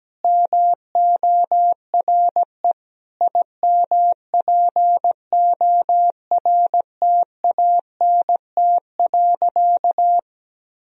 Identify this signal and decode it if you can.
Morse code, heard as a keyed tone, about 17 wpm, 700 Hz. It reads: MORE IMPORTANT.
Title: Sentences from Top 1000 Words - Review Part 3 - 17wpm